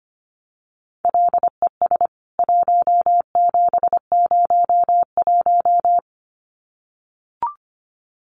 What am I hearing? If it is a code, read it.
Morse code, heard as a keyed tone, about 25 wpm, 700 Hz. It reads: LES 1701
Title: CWT Exchange 25wpm